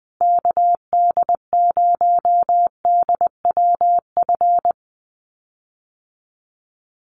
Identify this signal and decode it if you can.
Morse code, heard as a keyed tone, about 20 wpm, 700 Hz. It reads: KD0DWF
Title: USA Call Signs Repeated 2x 20wpm